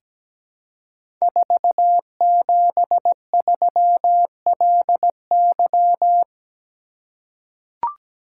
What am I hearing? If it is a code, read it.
Morse code, heard as a keyed tone, about 17 wpm, 700 Hz. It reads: 473LY
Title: Code Groups - 5 Letter:Numbers - 17wpm